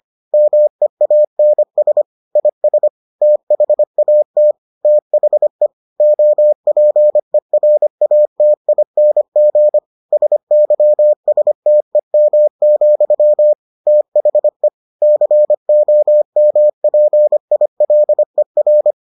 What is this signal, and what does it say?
Morse code, heard as a keyed tone, about 25 wpm, 600 Hz. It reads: MEANS IS THAT THE OPERATING SYSTEM, THE COMPILER